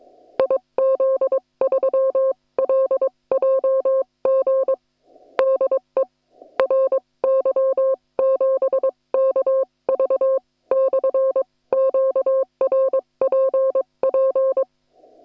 Morse code, heard as a keyed tone, about 22 wpm, spaced as for 19 wpm, 530 Hz. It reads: IZ3LJG DE RY7KV/QRPP
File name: attachments / hf 1 001